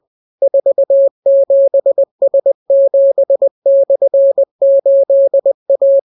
Morse code, heard as a keyed tone, about 20 wpm, 550 Hz. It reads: 47S7/8A